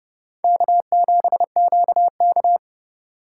Morse code, heard as a keyed tone, about 28 wpm, 700 Hz. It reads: K7QK